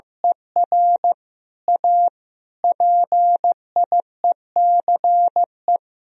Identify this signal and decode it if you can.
Morse code, heard as a keyed tone, about 16 wpm, 700 Hz. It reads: ER A PIECE
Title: Sentences from Top 1000 Words - Review Part 2 - 15wpm